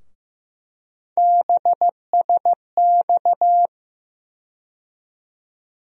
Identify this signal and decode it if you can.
Morse code, heard as a keyed tone, about 15 wpm, 700 Hz. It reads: BSX
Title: USA Call Sign Postfixes - 3 Letters 15wpm